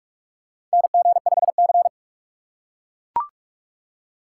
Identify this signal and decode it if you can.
Morse code, heard as a keyed tone, about 43 wpm, 700 Hz. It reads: NG5C
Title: USA Call Signs - 4 Characters - 2x Character Spacing - 45wpm